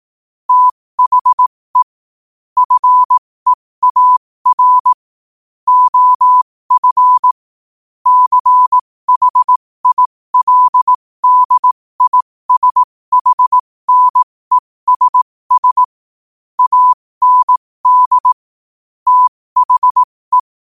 Morse code, heard as a keyed tone, about 18 wpm, spaced as for 15 wpm, 1000 Hz. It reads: THE FEAR OF CHILDISHNESS AND THE
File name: qotdfiles / 15wpm